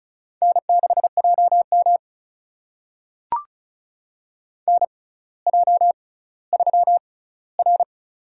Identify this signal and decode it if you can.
Morse code, heard as a keyed tone, about 36 wpm, spaced as for 18 wpm, 700 Hz. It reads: N6JM NJ3R
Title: USA Call Signs - 4 Characters - 6x Character Spacing - 35wpm